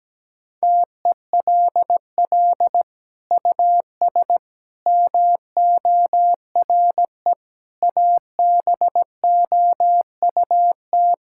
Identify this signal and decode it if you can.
Morse code, heard as a keyed tone, about 17 wpm, 700 Hz. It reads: TELL US MORE ABOUT
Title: Sentences from Top 1600 Words 17wpm